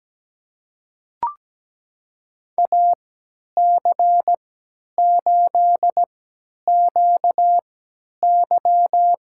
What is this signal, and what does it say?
Morse code, heard as a keyed tone, about 17 wpm, spaced as for 10 wpm, 700 Hz. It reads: AC8QY
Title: USA Call Signs - 3x Character Spacing - 17wpm